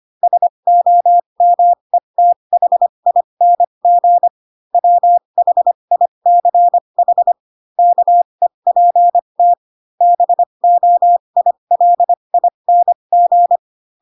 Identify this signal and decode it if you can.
Morse code, heard as a keyed tone, about 25 wpm, spaced as for 22 wpm, 700 Hz. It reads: SOMETHING WHICH KEPT BOILING
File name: Pinocchio / Pinocchio ch3 18 25 ik1hge